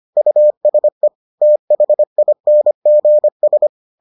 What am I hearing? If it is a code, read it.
Morse code, heard as a keyed tone, about 25 wpm, 600 Hz. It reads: USE THINGS